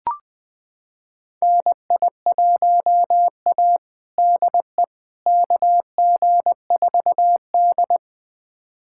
Morse code, heard as a keyed tone, about 20 wpm, 700 Hz. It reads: NI1A DE KG4D
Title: USA Callsign DE Callsign - 4 Characters - 20wpm